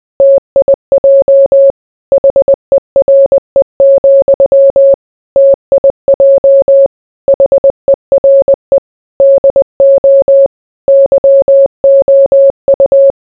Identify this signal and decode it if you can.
Morse code, heard as a keyed tone, about 20 wpm, 550 Hz. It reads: TIJ HERE, TIJ HERE DO YOU